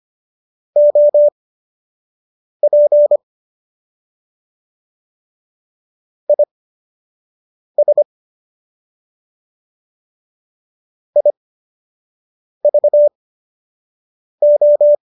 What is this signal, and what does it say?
Morse code, heard as a keyed tone, about 25 wpm, spaced as for 6 wpm, 590 Hz. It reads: OP IS IVO
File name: qso012 06